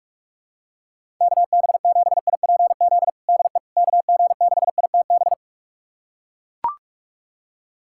Morse code, heard as a keyed tone, about 43 wpm, 700 Hz. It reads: KB7IPZ DE KG6ITB
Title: USA Callsign DE Callsign 45wpm